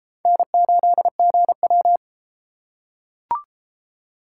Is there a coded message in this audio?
Yes: Morse code from 0.3 to 2.0 s, heard as a keyed tone, about 32 wpm, 700 Hz. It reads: N8GW